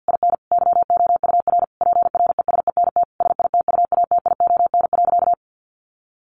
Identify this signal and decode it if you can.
Morse code, heard as a keyed tone, about 50 wpm, 700 Hz. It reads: IN YOUR PRESENT SITUATION.
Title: Sentences from Top 1200 Words 50wpm